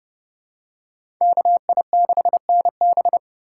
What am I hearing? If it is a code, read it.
Morse code, heard as a keyed tone, about 30 wpm, 700 Hz. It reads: KI6NB
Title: USA Call Signs - 5 Characters - 2x Character Spacing - 30wpm